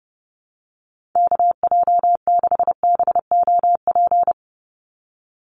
Morse code, heard as a keyed tone, about 30 wpm, 700 Hz. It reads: KJ6BOP
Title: USA Call Signs 30wpm